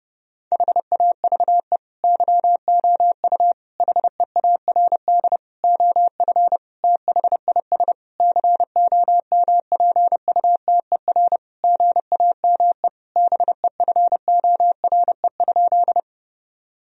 Morse code, heard as a keyed tone, about 30 wpm, 700 Hz. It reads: HAVE YOU HEARD OF THIS COMPUTER GAME BEFORE?
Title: Sentences from Top 1000 Words - Review Part 3 - 30wpm